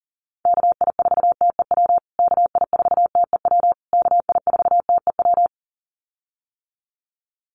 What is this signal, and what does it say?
Morse code, heard as a keyed tone, about 40 wpm, 700 Hz. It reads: KI4TEW KI4TEW KI4TEW
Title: USA Call Signs Repeated 3x 40wpm